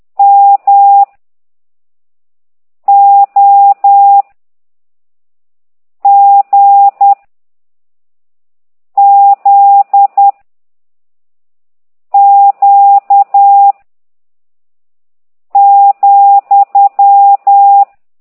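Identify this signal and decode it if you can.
Morse code, heard as a keyed tone, about 10 wpm, 800 Hz. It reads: M O G Z Q ,
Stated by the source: Morse code lesson 5 exercise 10 by CT2IDW, containing only characters M, O, G, Z, Q and comma.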